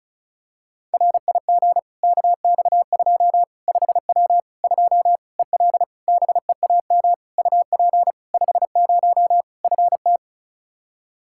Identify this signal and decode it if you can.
Morse code, heard as a keyed tone, about 36 wpm, 700 Hz. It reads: RIG KX2 5W 2 EL BEAM UP 50 FT
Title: Practice QSO - Exchange 2 - Info - 35wpm